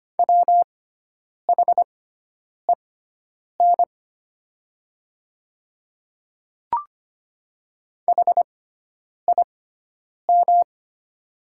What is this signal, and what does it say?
Morse code, heard as a keyed tone, about 25 wpm, spaced as for 9 wpm, 700 Hz. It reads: WHEN HIM
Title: 6x Character Spacing - Top 100 Words - 25wpm